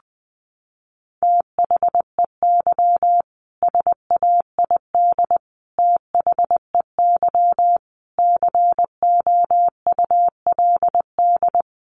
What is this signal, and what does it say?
Morse code, heard as a keyed tone, about 20 wpm, 700 Hz. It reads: THEY SAID THEY COULD
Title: Sentences from Top 1400 Words 20wpm